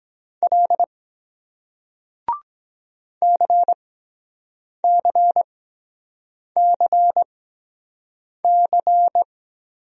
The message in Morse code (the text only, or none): L CCCC